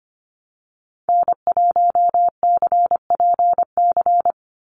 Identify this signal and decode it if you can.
Morse code, heard as a keyed tone, about 25 wpm, 700 Hz. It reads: N1CPC
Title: USA Call Signs 25wpm